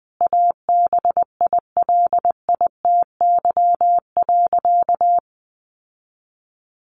Morse code, heard as a keyed tone, about 20 wpm, 700 Hz. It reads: ABILITY.
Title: Sentences from Top 1200 Words 20wpm